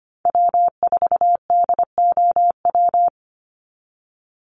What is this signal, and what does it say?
Morse code, heard as a keyed tone, about 25 wpm, 700 Hz. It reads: W4DOW
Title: NA SOTA Chaser Callsigns 25wpm